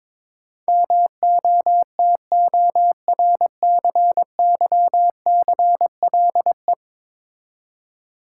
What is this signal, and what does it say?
Morse code, heard as a keyed tone, about 21 wpm, 700 Hz. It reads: MOTORCYCLE
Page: morse-code-ninja-practice-22wpm